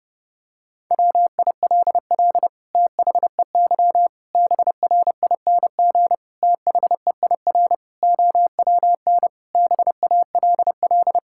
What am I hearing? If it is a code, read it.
Morse code, heard as a keyed tone, about 32 wpm, 700 Hz. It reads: WILL THEY BRING THEIR OWN BALL
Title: Sentences from Top 1000 Words - Review Part 6 - 30wpm